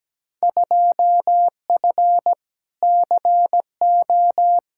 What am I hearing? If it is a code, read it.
Morse code, heard as a keyed tone, about 17 wpm, 700 Hz. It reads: 2F CO
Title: Field Day Exchanges - 17wpm